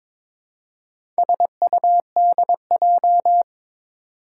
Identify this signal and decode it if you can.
Morse code, heard as a keyed tone, about 22 wpm, 700 Hz. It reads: SUDJ
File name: practice / 22wpm / Four Letter-Number Combinations - 22wpm